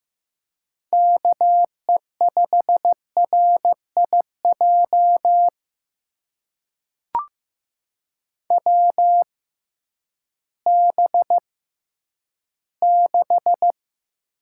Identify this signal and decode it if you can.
Morse code, heard as a keyed tone, about 15 wpm, spaced as for 10 wpm, 700 Hz. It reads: KE5RIJ WB6
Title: USA Call Signs - 6 Characters - 6x Character Spacing - 15wpm